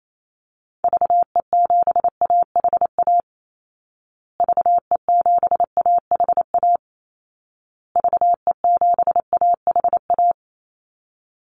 Morse code, heard as a keyed tone, about 28 wpm, 700 Hz. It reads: VE7AHA VE7AHA VE7AHA
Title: Call Signs Repeated 2x - 28wpm